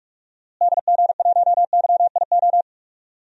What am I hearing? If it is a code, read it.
Morse code, heard as a keyed tone, about 45 wpm, 700 Hz. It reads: DG1YIO
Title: DXCC-100 Call Signs - 6 Characters - Repeated 2x - 45wpm